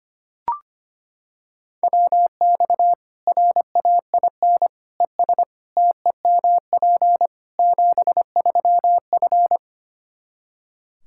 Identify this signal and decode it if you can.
Morse code, heard as a keyed tone, about 25 wpm, 700 Hz. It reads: WX RAIN ES TEMP 73F
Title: Practice QSO - Exchange 2 - Info - 25wpm